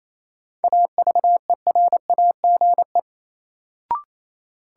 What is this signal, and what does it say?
Morse code, heard as a keyed tone, about 28 wpm, 700 Hz. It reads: AVERAGE
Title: Sentences from Top 1400 Words - New Word Drill Alt - 28wpm